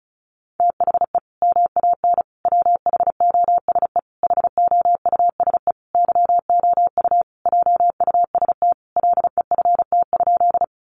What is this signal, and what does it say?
Morse code, heard as a keyed tone, about 34 wpm, 700 Hz. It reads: THE MAN WHOSE HOUSE YOU JUST LEFT?
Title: Sentences from Top 1200 Words 35wpm